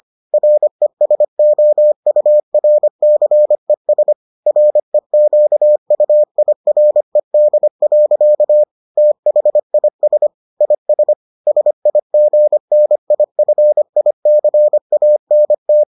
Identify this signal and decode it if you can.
Morse code, heard as a keyed tone, about 25 wpm, 600 Hz. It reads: RESOURCES REQUIRED. THIS IS SIGNIFICANT